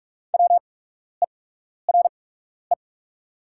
Morse code, heard as a keyed tone, about 48 wpm, spaced as for 12 wpm, 690 Hz. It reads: WERE
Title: 8x Character Spacing - Top 100 Words - 45wpm